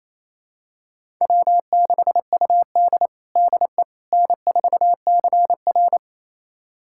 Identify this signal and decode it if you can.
Morse code, heard as a keyed tone, about 28 wpm, 700 Hz. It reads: W6UD DE N4CR
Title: USA Callsign DE Callsign - 4 Characters - 28wpm